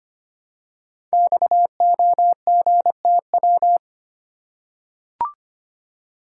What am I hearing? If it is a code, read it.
Morse code, heard as a keyed tone, about 25 wpm, 700 Hz. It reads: XOGTW
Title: Code Groups - 5 Letter:Numbers - 25wpm